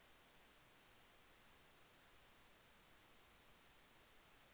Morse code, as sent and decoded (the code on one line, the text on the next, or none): none